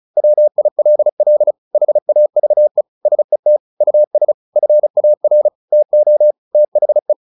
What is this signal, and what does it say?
Morse code, heard as a keyed tone, about 34 wpm, 600 Hz. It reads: WILL HAVE SET US FAR TO THE
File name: the old man and the sea / the old man and the sea 1 35wpm